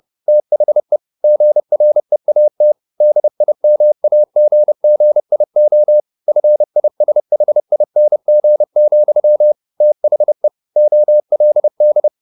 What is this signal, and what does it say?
Morse code, heard as a keyed tone, about 30 wpm, 600 Hz. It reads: THE GREAT DIMAGGIO FISHING, THE OLD
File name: the old man and the sea / the old man and the sea 1 30wpm